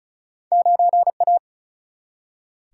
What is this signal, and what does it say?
Morse code, heard as a keyed tone, about 36 wpm, 700 Hz. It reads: 9 A
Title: Single-Letter Single-Number 35wpm